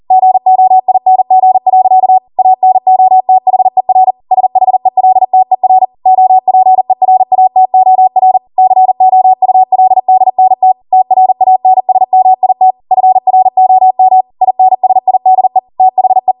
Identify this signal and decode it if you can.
Morse code, heard as a keyed tone, about 40 wpm, 750 Hz. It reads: GOING. ANOTHER SHELTER OPERATOR COULDNT TRANSMIT FROM INSIDE THE